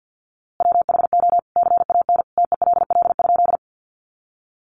Morse code, heard as a keyed tone, about 50 wpm, 700 Hz. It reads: WHO CAN TELL?
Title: A Sound Of Thunder 50wpm